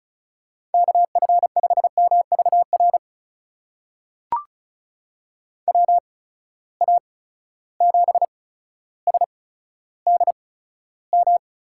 Morse code, heard as a keyed tone, about 36 wpm, spaced as for 15 wpm, 700 Hz. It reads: KF5MVR WA7SDM
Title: USA Call Signs - 6 Characters - 8x Character Spacing - 35wpm